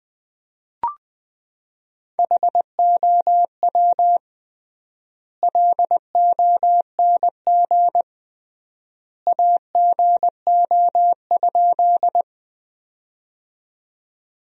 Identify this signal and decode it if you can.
Morse code, heard as a keyed tone, about 20 wpm, 700 Hz. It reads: HOW LONG AGO?
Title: 3x Word-Spacing - Sentences from Top 500 Words - 20wpm